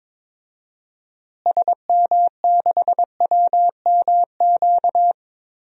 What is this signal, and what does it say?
Morse code, heard as a keyed tone, about 22 wpm, 700 Hz. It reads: SM6WMQ
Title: DXCC-100 Call Signs - 6 Characters - Repeated 3x - 22wpm